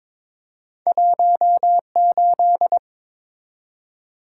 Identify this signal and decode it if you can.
Morse code, heard as a keyed tone, about 22 wpm, 700 Hz. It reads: 18
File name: practice / 22wpm / Two Letter-Number Combinations - 22wpm